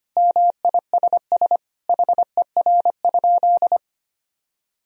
Morse code, heard as a keyed tone, about 25 wpm, 700 Hz. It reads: MISS HER?
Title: Sentences from Top 1000 Words - Review Part 3 - 25wpm